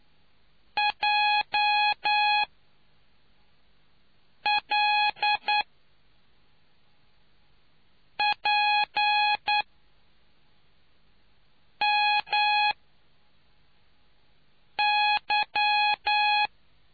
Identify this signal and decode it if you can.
Morse code, heard as a keyed tone, about 9 wpm, 830 Hz. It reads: J L P M Y